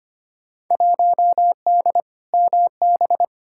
Morse code, heard as a keyed tone, about 25 wpm, 700 Hz. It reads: 1D MB
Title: Field Day Exchanges Encore - 25wpm